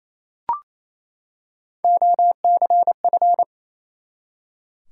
Morse code, heard as a keyed tone, about 28 wpm, 700 Hz. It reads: OCF
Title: Top 180 QSO Elements 28wpm